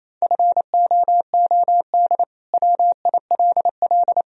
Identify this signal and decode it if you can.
Morse code, heard as a keyed tone, about 28 wpm, 700 Hz. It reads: FOOD WILL
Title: Sentences from Top 1000 Words - Review Part 2 - 28wpm